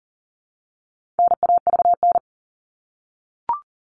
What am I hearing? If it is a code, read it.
Morse code, heard as a keyed tone, about 40 wpm, 700 Hz. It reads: NAVN
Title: All Cut Numbers - 4-Digit 40wpm